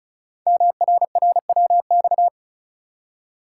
Morse code, heard as a keyed tone, about 36 wpm, 700 Hz. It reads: MRRWX
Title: Code Groups - 5 Letters - 35wpm